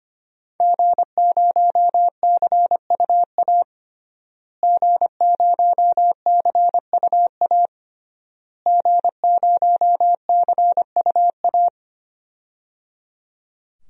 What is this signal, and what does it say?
Morse code, heard as a keyed tone, about 25 wpm, 700 Hz. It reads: G0CUA G0CUA G0CUA
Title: DXCC-100 Call Signs - 5 Characters - Repeated 3x - 25wpm